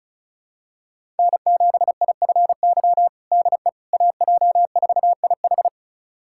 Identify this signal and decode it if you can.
Morse code, heard as a keyed tone, about 36 wpm, 700 Hz. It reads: N7IFY DE AJ4IH
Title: USA Callsign DE Callsign - 5 Characters - 35wpm